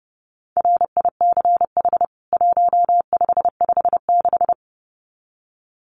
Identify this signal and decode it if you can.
Morse code, heard as a keyed tone, about 30 wpm, 700 Hz. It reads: RICH 1556
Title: CWT Exchange 30wpm